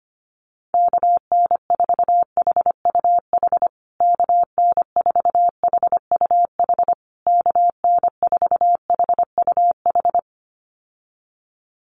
Morse code, heard as a keyed tone, about 25 wpm, 700 Hz. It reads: KN4HUH KN4HUH KN4HUH
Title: USA Call Signs Repeated 3x 25wpm